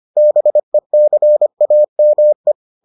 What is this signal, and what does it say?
Morse code, heard as a keyed tone, about 25 wpm, 600 Hz. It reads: BECAME